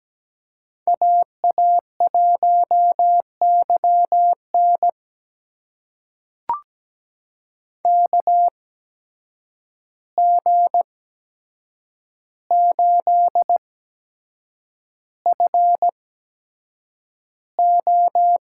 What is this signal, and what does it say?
Morse code, heard as a keyed tone, about 17 wpm, spaced as for 7 wpm, 700 Hz. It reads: AA1YN KG8FO
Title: USA Call Signs - 5 Characters - 8x Character Spacing - 17wpm